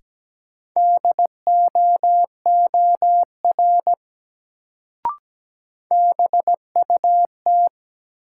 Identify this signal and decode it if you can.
Morse code, heard as a keyed tone, about 17 wpm, 700 Hz. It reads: DOOR BUT
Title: Sets of 2 Words - New Word Drill - 17wpm